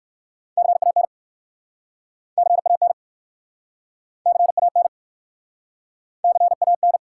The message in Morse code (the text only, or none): CAN CAN CAN CAN